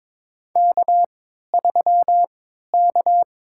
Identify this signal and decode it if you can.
Morse code, heard as a keyed tone, about 22 wpm, spaced as for 13 wpm, 700 Hz. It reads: K3K
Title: USA Call Signs - 3x Character Spacing - 22wpm